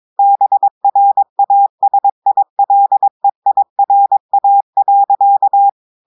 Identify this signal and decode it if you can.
Morse code, heard as a keyed tone, about 22 wpm, 800 Hz. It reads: BRASILEIRA.
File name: audios / portugais / portugais texte en clair 1 vitesse 22 mots minutes